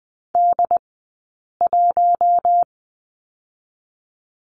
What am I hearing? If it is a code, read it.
Morse code, heard as a keyed tone, about 20 wpm, 700 Hz. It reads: D 1
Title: 066 Lesson 12d - Letter d - Two Characters 20wpm